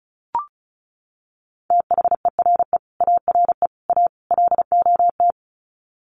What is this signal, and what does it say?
Morse code, heard as a keyed tone, about 34 wpm, 700 Hz. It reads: THERE ARE A LOT
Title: Sets of 4 Words 35wpm